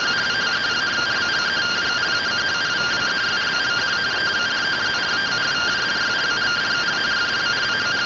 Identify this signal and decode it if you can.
Two-tone teleprinter signal, about 170 Hz shift.